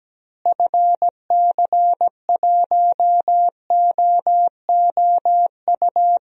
Morse code, heard as a keyed tone, about 17 wpm, 700 Hz. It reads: FC1OOU
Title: DXCC-100 Call Signs - 6 Characters - Repeated 3x - 17wpm